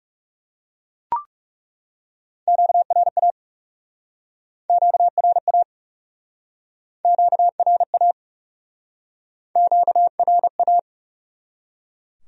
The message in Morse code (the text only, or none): QRA QRA QRA QRA